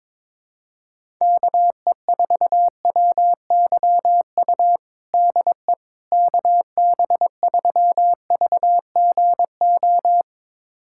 Morse code, heard as a keyed tone, about 21 wpm, 700 Hz. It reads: KE4WYU DE KB3VGO
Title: USA Callsign DE Callsign - 6 Characters - 22wpm